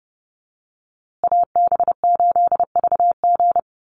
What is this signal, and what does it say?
Morse code, heard as a keyed tone, about 30 wpm, 700 Hz. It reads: AB8VG
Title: USA Call Signs Repeated 3x 30wpm